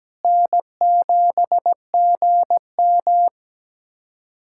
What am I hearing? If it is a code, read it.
Morse code, heard as a keyed tone, about 17 wpm, 700 Hz. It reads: N7GM
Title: USA Callsign DE Callsign - 4 Characters - 17wpm